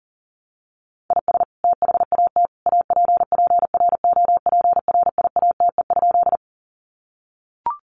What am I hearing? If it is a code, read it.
Morse code, heard as a keyed tone, about 40 wpm, 700 Hz. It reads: IS THAT APPROPRIATE?
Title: Sentences from Top 900 Words 40wpm